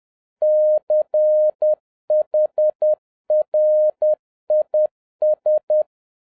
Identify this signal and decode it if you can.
Morse code, heard as a keyed tone, about 10 wpm, 610 Hz. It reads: CHRIS